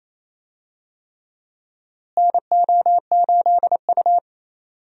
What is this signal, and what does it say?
Morse code, heard as a keyed tone, about 27 wpm, 700 Hz. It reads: NO8U
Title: USA Call Signs - 4 Characters - Repeated 3x - 28wpm